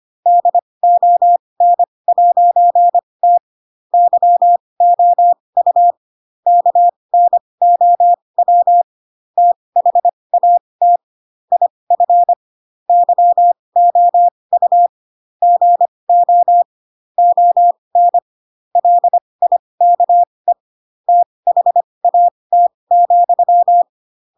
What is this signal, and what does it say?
Morse code, heard as a keyed tone, about 25 wpm, spaced as for 20 wpm, 700 Hz. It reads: DON'T YOU KNOW THAT IF YOU GO ON LIKE THAT,
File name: Pinocchio / Pinocchio ch4 15 25 ik1hge